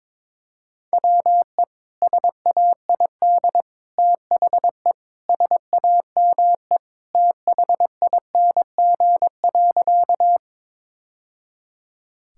Morse code, heard as a keyed tone, about 22 wpm, 700 Hz. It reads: WE SAID THE SAME THING.
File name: practice / 22wpm / Sentences with 3-Word Phrases - 22wpm